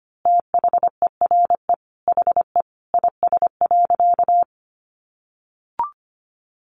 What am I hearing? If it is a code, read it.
Morse code, heard as a keyed tone, about 25 wpm, 700 Hz. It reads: THERE HE IS.
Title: A Sound Of Thunder 25wpm